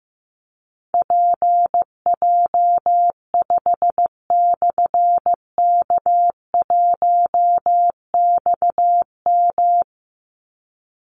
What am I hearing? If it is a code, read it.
Morse code, heard as a keyed tone, about 15 wpm, 700 Hz. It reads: PJ5/K1XM
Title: Call Signs Repeated 3x - 15wpm